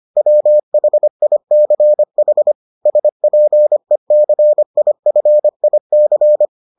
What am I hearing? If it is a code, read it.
Morse code, heard as a keyed tone, about 25 wpm, 600 Hz. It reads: WHICH SPECIFIC